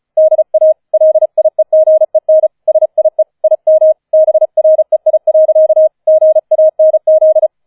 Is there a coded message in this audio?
Yes: Morse code from 0.2 to 7.5 s, heard as a keyed tone, about 34 wpm, 630 Hz. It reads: DA LIEGEN SIE IM BREI. GANZ